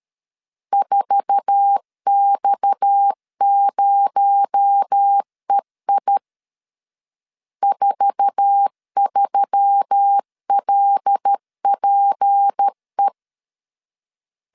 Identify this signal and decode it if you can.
Morse code, heard as a keyed tone, about 13 wpm, 800 Hz. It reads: 4X0EI 43LPE
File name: LEZIONI / 8 YJ45 / 8 YJ4 12wpm 2151